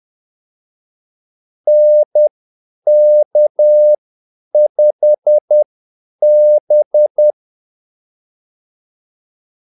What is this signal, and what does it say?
Morse code, heard as a keyed tone, about 10 wpm, spaced as for 8 wpm, 600 Hz. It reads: NK5B
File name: Callsigns-10-8wpm